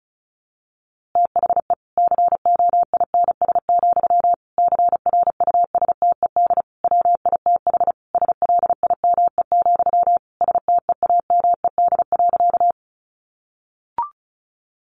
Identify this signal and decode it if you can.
Morse code, heard as a keyed tone, about 34 wpm, 700 Hz. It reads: THE COINS, CRUSTED WITH SLIME, STEAMED.
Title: A Sound Of Thunder 35wpm